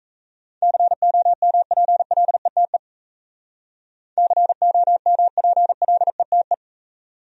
Morse code, heard as a keyed tone, about 34 wpm, 700 Hz. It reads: COMPLETE COMPLETE
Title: Speed-Racing Top 500 Words 28wpm